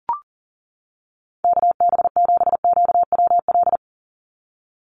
Morse code, heard as a keyed tone, about 38 wpm, 690 Hz. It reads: KB7QWL
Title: USA Call Signs 40wpm